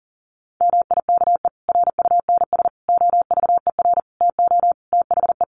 Morse code, heard as a keyed tone, about 37 wpm, 700 Hz. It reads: MIKE RUNS OVER TO THE
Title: Additional Easy Phrases 40wpm